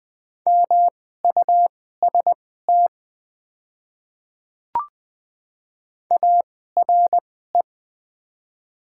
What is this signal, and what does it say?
Morse code, heard as a keyed tone, about 20 wpm, spaced as for 14 wpm, 700 Hz. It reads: MUST ARE
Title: 2x Character Spacing - Top 200 Words - 20wpm